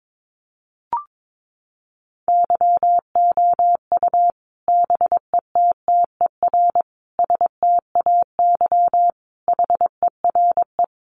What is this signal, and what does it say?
Morse code, heard as a keyed tone, about 22 wpm, 700 Hz. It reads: YOU BETTER STAY HERE
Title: Sentences from Top 500 Words 22wpm